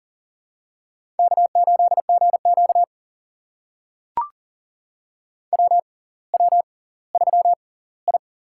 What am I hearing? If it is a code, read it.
Morse code, heard as a keyed tone, about 40 wpm, spaced as for 20 wpm, 700 Hz. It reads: K8GQ WW3I
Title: USA Call Signs - 4 Characters - 6x Character Spacing - 40wpm